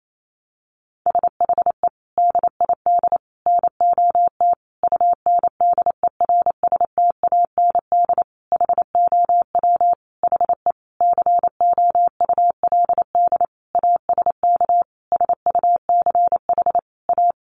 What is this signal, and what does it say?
Morse code, heard as a keyed tone, about 28 wpm, 700 Hz. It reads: SHE DID NOT UNDERSTAND HOW HE COULD ASK SUCH A
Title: Sentences from Top 600 Words 28wpm